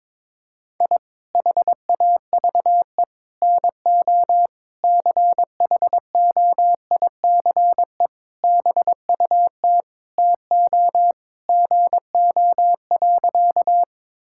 Morse code, heard as a keyed tone, about 21 wpm, 700 Hz. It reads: I HAVE NO CHOICE BUT TO GO.
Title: Sentences from Top 1000 Words - Review Part 6 - 22wpm